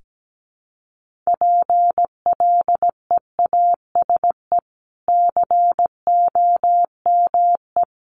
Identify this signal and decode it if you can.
Morse code, heard as a keyed tone, about 17 wpm, 700 Hz. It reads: PLEASE COME
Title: Sentences from Top 1500 Words 17wpm